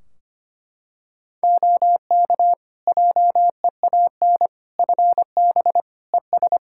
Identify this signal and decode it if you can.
Morse code, heard as a keyed tone, about 25 wpm, 700 Hz. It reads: OK JEAN FB ES
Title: Practice QSO - Exchange 2 - Info - 25wpm